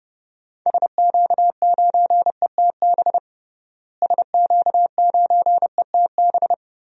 Morse code, heard as a keyed tone, about 31 wpm, 700 Hz. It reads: SQ9ETB SQ9ETB
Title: DXCC-100 Call Signs - 6 Characters - Repeated 2x - 30wpm